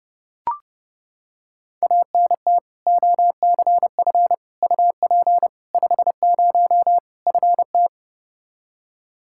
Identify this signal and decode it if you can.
Morse code, heard as a keyed tone, about 32 wpm, 700 Hz. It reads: ANT OCF UP 50 FT
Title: Practice QSO - Exchange 2 - Info - 30wpm